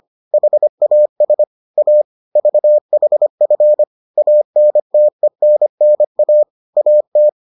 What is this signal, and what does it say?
Morse code, heard as a keyed tone, about 25 wpm, 600 Hz. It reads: HAS A VHF ANTENNA AT